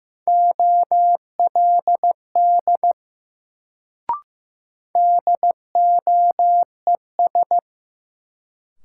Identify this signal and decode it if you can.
Morse code, heard as a keyed tone, about 15 wpm, 700 Hz. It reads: OLD DOES
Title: Sets of 2 Words - New Word Drill - 15wpm